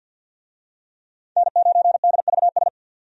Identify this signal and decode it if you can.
Morse code, heard as a keyed tone, about 52 wpm, 700 Hz. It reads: N9DVS
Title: USA Call Signs - 5 Characters - 50wpm